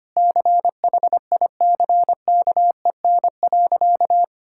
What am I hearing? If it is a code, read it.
Morse code, heard as a keyed tone, about 25 wpm, 700 Hz. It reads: CHICKEN.
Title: Sentences from Top 1000 Words - Review Part 5 - 25wpm